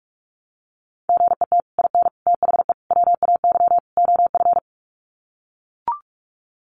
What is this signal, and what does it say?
Morse code, heard as a keyed tone, about 44 wpm, 700 Hz. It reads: GET IN THE WAY OF
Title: Sets of 5 Words Encore 45wpm